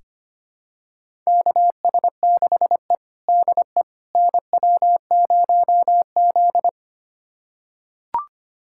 Morse code, heard as a keyed tone, about 25 wpm, 700 Hz. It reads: KS6E DE NW0Z
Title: USA Callsign DE Callsign - 4 Characters - 25wpm